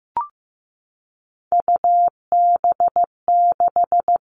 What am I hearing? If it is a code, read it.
Morse code, heard as a keyed tone, about 15 wpm, 700 Hz. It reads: UB6
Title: All Cut Numbers - 3-Digit 15wpm